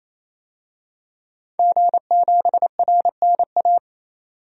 Morse code, heard as a keyed tone, about 28 wpm, 700 Hz. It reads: G7RNA
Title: DXCC-100 Call Signs - 5 Characters - 28wpm